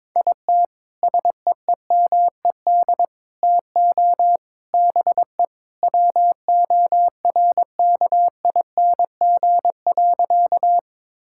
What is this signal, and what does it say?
Morse code, heard as a keyed tone, about 21 wpm, 700 Hz. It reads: IT SEEMED TO BE WORKING.
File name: practice / 22wpm / Sentences with 3-Word Phrases - 22wpm